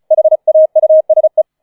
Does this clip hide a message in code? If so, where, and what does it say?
Yes: Morse code from 0.1 to 1.4 s, heard as a keyed tone, about 34 wpm, 630 Hz. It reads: HAUSE